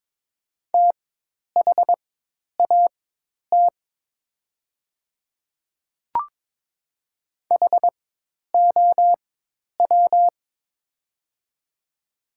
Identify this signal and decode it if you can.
Morse code, heard as a keyed tone, about 22 wpm, spaced as for 10 wpm, 700 Hz. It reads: THAT HOW
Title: 4x Character Spacing - Top 100 Words - 22wpm